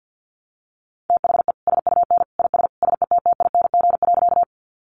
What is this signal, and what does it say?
Morse code, heard as a keyed tone, about 48 wpm, 700 Hz. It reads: THE SUN IS SETTING.
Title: Sentences from Top 1300 Words 50wpm